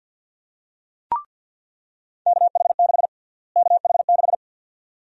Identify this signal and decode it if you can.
Morse code, heard as a keyed tone, about 48 wpm, 700 Hz. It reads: KH6 KH6
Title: USA Call Sign Prefixes - Repeated 2x - 50wpm